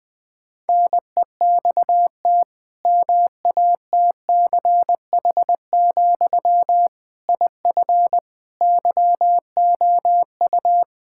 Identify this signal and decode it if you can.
Morse code, heard as a keyed tone, about 20 wpm, 700 Hz. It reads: NEXT MATCH, IF YOU
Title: Sentences from Top 1600 Words 20wpm